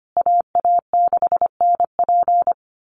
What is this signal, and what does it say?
Morse code, heard as a keyed tone, about 25 wpm, 700 Hz. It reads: AA6NP
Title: USA Call Signs Repeated 3x 25wpm